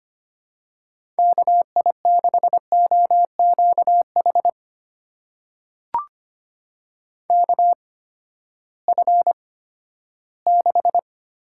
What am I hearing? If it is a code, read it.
Morse code, heard as a keyed tone, about 25 wpm, spaced as for 14 wpm, 700 Hz. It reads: KI6OQH KF6
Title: USA Call Signs - 6 Characters - 8x Character Spacing - 25wpm